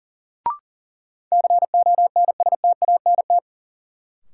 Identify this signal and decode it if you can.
Morse code, heard as a keyed tone, about 40 wpm, 700 Hz. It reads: CONSTANT
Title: Sentences from Top 1500 Words - New Word Drill - 40wpm